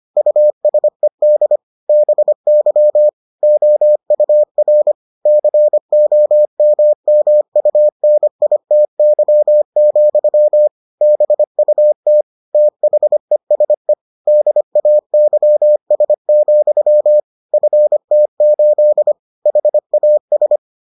Morse code, heard as a keyed tone, about 25 wpm, 600 Hz. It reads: USED BY OUR COMMUNITY, BUT THESE DAYS, FT8 HAS